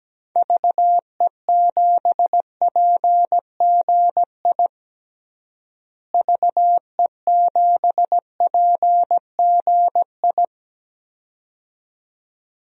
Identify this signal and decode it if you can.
Morse code, heard as a keyed tone, about 17 wpm, 700 Hz. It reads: VE7PGI VE7PGI
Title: DXCC-100 Call Signs - 6 Characters - Repeated 2x - 17wpm